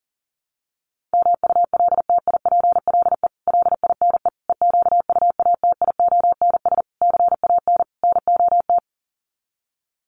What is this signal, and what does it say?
Morse code, heard as a keyed tone, about 40 wpm, 700 Hz. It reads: MULTIPLE LINE EQUATIONS CAN NOT
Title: Sets of 5 Words 40wpm